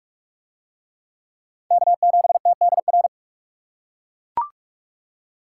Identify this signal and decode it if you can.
Morse code, heard as a keyed tone, about 41 wpm, 700 Hz. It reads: K7TDR